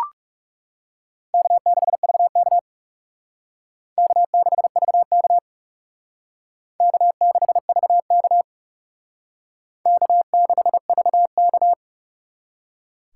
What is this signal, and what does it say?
Morse code, heard as a keyed tone, about 37 wpm, 700 Hz. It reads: K6VK K6VK K6VK K6VK
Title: Speed Racing - USA 1x2 Call Signs 30wpm